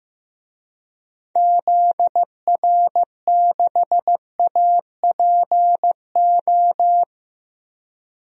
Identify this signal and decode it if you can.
Morse code, heard as a keyed tone, about 15 wpm, 700 Hz. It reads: ZR6APO
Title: DXCC-100 Call Signs - 6 Characters - Repeated 3x - 15wpm